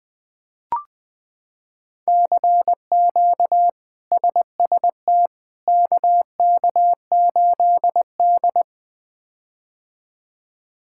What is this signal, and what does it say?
Morse code, heard as a keyed tone, about 21 wpm, 700 Hz. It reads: CQ SST KK8D
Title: K1USN SST QSO Practice - 20wpm